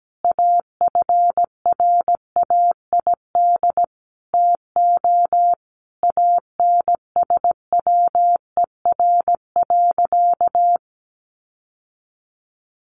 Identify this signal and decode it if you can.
Morse code, heard as a keyed tone, about 17 wpm, 700 Hz. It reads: AFRAID TO ANSWER.
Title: Sentences from Top 1500 Words 17wpm